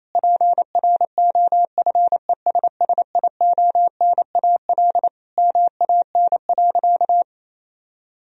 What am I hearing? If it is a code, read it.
Morse code, heard as a keyed tone, about 28 wpm, 700 Hz. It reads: PROFESSIONAL MAN.
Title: Sentences from Top 1000 Words - Review Part 4 - 28wpm